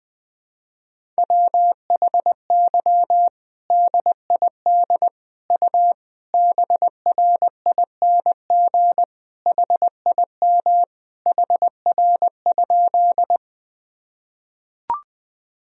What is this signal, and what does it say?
Morse code, heard as a keyed tone, about 21 wpm, 700 Hz. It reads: WHY DID U BRING HIM HR?
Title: Sentences from Top 500 Words and Abbreviations 20wpm